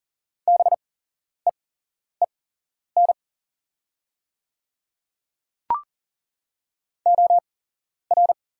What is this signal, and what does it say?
Morse code, heard as a keyed tone, about 40 wpm, spaced as for 11 wpm, 700 Hz. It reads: BEEN OR
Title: 8x Character Spacing - Top 100 Words - 40wpm